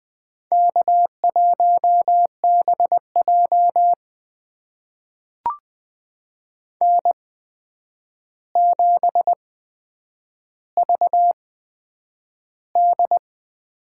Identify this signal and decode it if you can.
Morse code, heard as a keyed tone, about 20 wpm, spaced as for 9 wpm, 700 Hz. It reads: K1BJ N7VD
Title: USA Call Signs - 4 Characters - 8x Character Spacing - 20wpm